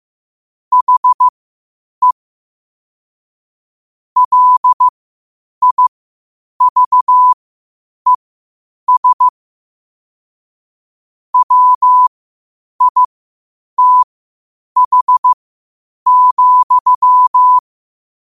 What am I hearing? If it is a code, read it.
Morse code, heard as a keyed tone, about 15 wpm, spaced as for 8 wpm, 1000 Hz. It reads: HE LIVES WITH,
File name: qotdfiles / 8wpm